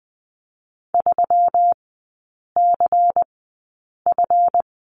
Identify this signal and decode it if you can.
Morse code, heard as a keyed tone, about 20 wpm, spaced as for 8 wpm, 700 Hz. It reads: 3CF
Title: 138 Lesson 23e - Letter f - Three Characters 20wpm